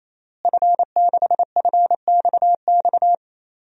Morse code, heard as a keyed tone, about 28 wpm, 700 Hz. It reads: F6FXX
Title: DXCC-100 Call Signs - 5 Characters - Repeated 3x - 28wpm